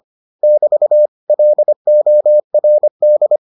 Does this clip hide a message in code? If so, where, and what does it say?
Yes: Morse code from 0.4 to 3.4 s, heard as a keyed tone, about 25 wpm, 600 Hz. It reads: <BT> LORD